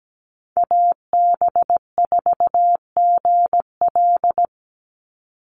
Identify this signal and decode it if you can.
Morse code, heard as a keyed tone, about 17 wpm, 700 Hz. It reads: AB4GL
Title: USA Call Signs 17wpm